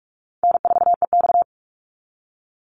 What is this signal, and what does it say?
Morse code, heard as a keyed tone, about 44 wpm, 700 Hz. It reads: N4EX
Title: NA SOTA Activator Callsigns 45wpm